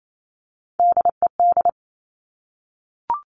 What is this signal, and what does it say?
Morse code, heard as a keyed tone, about 28 wpm, 700 Hz. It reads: DED
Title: Top Three Letter Combinations in Words 28wpm